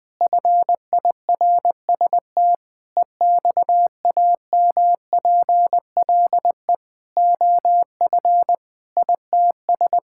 Morse code, heard as a keyed tone, about 21 wpm, 700 Hz. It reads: FIRST EXAMPLE OF ITS
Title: Sentences with 6-Word Phrases - Spoken Beginning - 20wpm